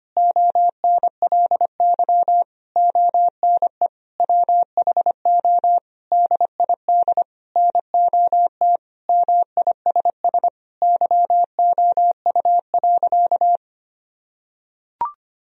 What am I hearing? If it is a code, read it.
Morse code, heard as a keyed tone, about 25 wpm, 700 Hz. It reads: ONLY ONE WHO DID NOT MISS YOU.
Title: Sentences from Top 1000 Words - Review Part 3 - 25wpm